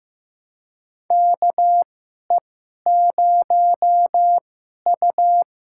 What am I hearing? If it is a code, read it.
Morse code, heard as a keyed tone, about 15 wpm, spaced as for 11 wpm, 700 Hz. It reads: KE0U
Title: USA Call Signs - 6 Characters - 2x Character Spacing - 15wpm